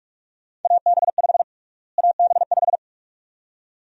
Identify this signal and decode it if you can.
Morse code, heard as a keyed tone, about 42 wpm, 700 Hz. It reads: AB5 AB5
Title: USA Call Sign Prefixes - Repeated 2x - 45wpm